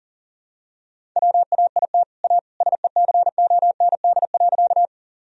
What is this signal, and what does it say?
Morse code, heard as a keyed tone, about 40 wpm, 700 Hz. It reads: WAIT A SECOND.
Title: Sentences from Top 1000 Words - Review Part 1 - 40wpm